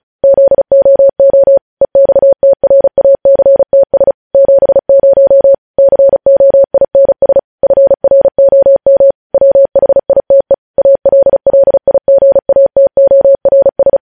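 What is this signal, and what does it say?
Morse code, heard as a keyed tone, about 36 wpm, 550 Hz. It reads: ZOO EXTRACTS 70 COINS FROM WHITE ALLIGATORS
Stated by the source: News headlines sent in Morse at 35 WPM.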